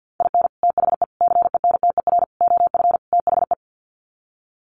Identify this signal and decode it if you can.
Morse code, heard as a keyed tone, about 48 wpm, 700 Hz. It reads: IN THE CENTER OF THE
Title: Sets of 5 Words 50wpm